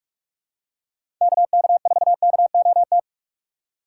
Morse code, heard as a keyed tone, about 46 wpm, 700 Hz. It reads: KK4KOT
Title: USA Call Signs - 2x Character Spacing - 45wpm